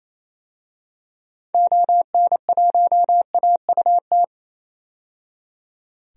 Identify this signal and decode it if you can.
Morse code, heard as a keyed tone, about 28 wpm, 700 Hz. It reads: ON1AUT
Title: DXCC-100 Call Signs - 6 Characters - 28wpm